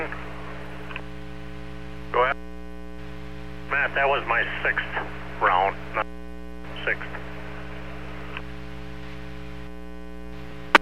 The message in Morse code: none

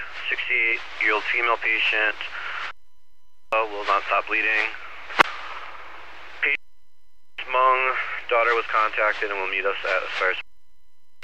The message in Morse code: none

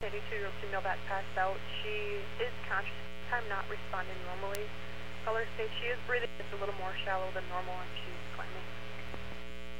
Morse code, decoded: none

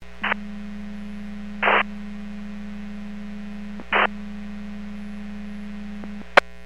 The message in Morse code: none

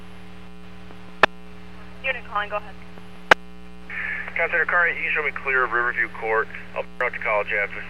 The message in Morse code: none

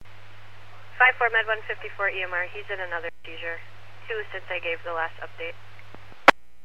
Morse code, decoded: none

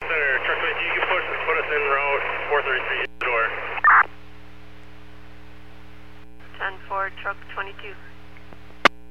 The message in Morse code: none

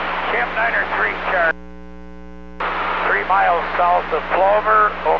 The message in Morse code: none

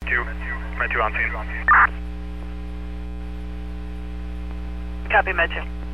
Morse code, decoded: none

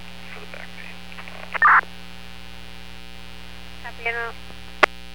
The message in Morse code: none

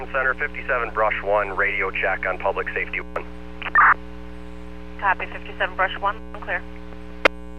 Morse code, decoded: none

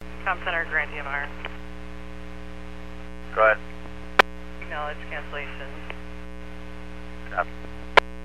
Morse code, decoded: none

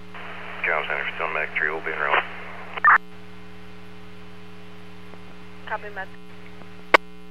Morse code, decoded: none

